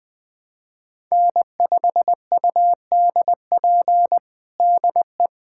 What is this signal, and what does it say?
Morse code, heard as a keyed tone, about 20 wpm, 700 Hz. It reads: N5UDP DE
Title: USA Callsign DE Callsign - 5 Characters - 20wpm